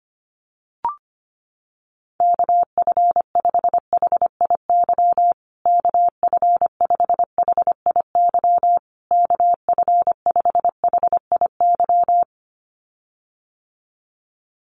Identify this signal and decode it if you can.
Morse code, heard as a keyed tone, about 25 wpm, 700 Hz. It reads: KF5HIY KF5HIY KF5HIY
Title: USA Call Signs Repeated 3x 25wpm